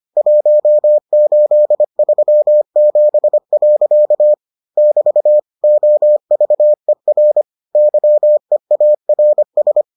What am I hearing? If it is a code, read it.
Morse code, heard as a keyed tone, about 25 wpm, 600 Hz. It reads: 1837. <BT> OVER YEARS